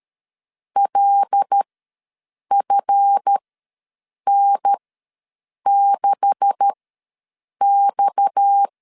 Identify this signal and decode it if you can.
Morse code, heard as a keyed tone, about 13 wpm, spaced as for 7 wpm, 800 Hz. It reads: LFN6X
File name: LEZIONI / 10 SIGNS / 10A SLASH ws 12wpm 3443